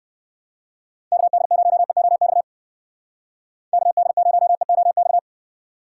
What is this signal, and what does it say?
Morse code, heard as a keyed tone, about 54 wpm, 700 Hz. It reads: KD0EOX KD0EOX
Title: Speed Racing - USA Call Signs 45wpm